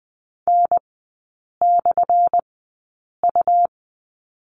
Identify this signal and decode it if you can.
Morse code, heard as a keyed tone, about 20 wpm, spaced as for 8 wpm, 700 Hz. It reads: N/U
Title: 178 Lesson 29e - Slash - Three Characters 20wpm